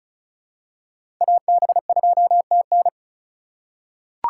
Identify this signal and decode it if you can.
Morse code, heard as a keyed tone, about 34 wpm, 700 Hz. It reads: AB2TN
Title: USA Call Signs - 5 Characters - 2x Character Spacing - 35wpm